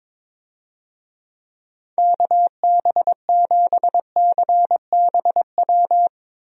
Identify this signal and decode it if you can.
Morse code, heard as a keyed tone, about 23 wpm, 700 Hz. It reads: KB7CBW
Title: USA Call Signs - 6 Characters - Repeated 3x - 22wpm